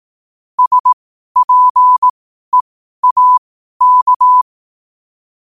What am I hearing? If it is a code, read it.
Morse code, heard as a keyed tone, about 18 wpm, spaced as for 12 wpm, 1000 Hz. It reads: SPEAK
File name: qotdfiles / 12wpm